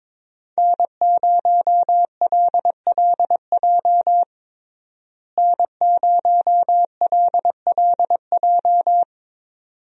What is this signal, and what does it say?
Morse code, heard as a keyed tone, about 22 wpm, 700 Hz. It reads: N0LLJ N0LLJ
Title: USA Call Signs - 5 Characters - Repeated 2x - 22wpm